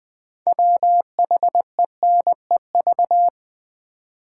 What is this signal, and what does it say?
Morse code, heard as a keyed tone, about 20 wpm, 700 Hz. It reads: WHENEV
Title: Build A Word - 2 Letters At A Time - 8-Letter Words - 20wpm